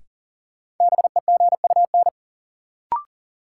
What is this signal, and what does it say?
Morse code, heard as a keyed tone, about 40 wpm, 700 Hz. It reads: BEGUN
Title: Sentences from Top 1000 Words - Review Part 6 - New Word Drill - 40wpm